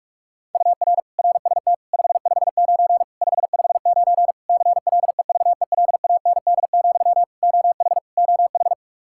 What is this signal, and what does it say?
Morse code, heard as a keyed tone, about 46 wpm, 700 Hz. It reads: UR RST 559 559 CLEVELAND, OH OH
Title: K1USN SST QSO Practice 45wpm